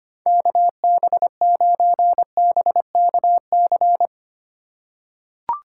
Morse code, heard as a keyed tone, about 25 wpm, 700 Hz. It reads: KB9BKC
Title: USA Call Signs - 6 Characters - 3x Character Spacing - 25wpm